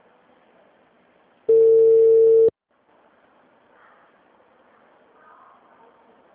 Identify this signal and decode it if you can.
Ringback tone (China, 450 Hz).
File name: out- -2001- - - .267